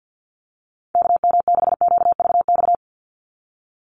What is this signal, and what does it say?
Morse code, heard as a keyed tone, about 50 wpm, 700 Hz. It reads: KM6QVX
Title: USA Call Signs 50wpm